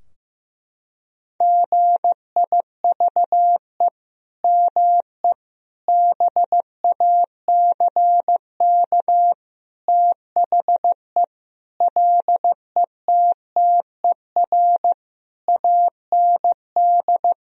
Morse code, heard as a keyed tone, about 15 wpm, 700 Hz. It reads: GIVE ME BACK THE LETTER AND
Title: Sentences from Top 1000 Words - Review Part 4 - 15wpm